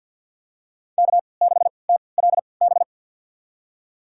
Morse code, heard as a keyed tone, about 48 wpm, spaced as for 28 wpm, 700 Hz. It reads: K6TLB
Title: USA Call Signs - 5 Characters - 3x Character Spacing - 50wpm